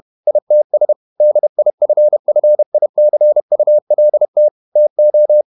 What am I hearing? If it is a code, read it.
Morse code, heard as a keyed tone, about 31 wpm, 600 Hz. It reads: ITS DIFFICULT TO
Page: Never 31wpm | CW med Gnister